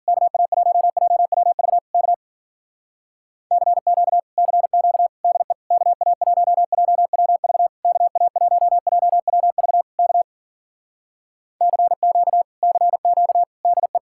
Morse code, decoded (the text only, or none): KA1JWV K CQ CQ DE KA1JWV KA1JWV K CQ CQ DE